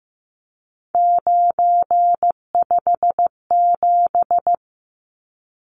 Morse code, heard as a keyed tone, about 15 wpm, 700 Hz. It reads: 957
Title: Numbers - 3-Digit 15wpm